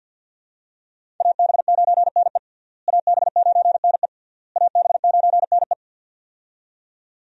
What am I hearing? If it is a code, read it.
Morse code, heard as a keyed tone, about 49 wpm, 700 Hz. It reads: AB9NE AB9NE AB9NE
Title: USA Call Signs - 5 Characters - Repeated 3x - 50wpm